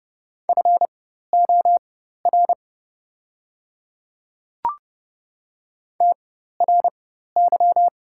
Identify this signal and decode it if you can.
Morse code, heard as a keyed tone, about 30 wpm, spaced as for 14 wpm, 690 Hz. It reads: FOR TRY
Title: 4x Character Spacing - Top 300 Words - 30wpm